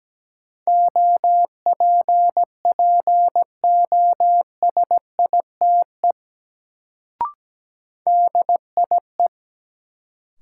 Morse code, heard as a keyed tone, about 17 wpm, 700 Hz. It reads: OPPOSITE DIE
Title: Sentences from Top 1300 Words - New Word Drill - 17wpm